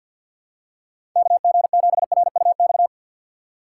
Morse code, heard as a keyed tone, about 52 wpm, 700 Hz. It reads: KG7RUX
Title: USA Call Signs - 6 Characters - 4x Character Spacing - 50wpm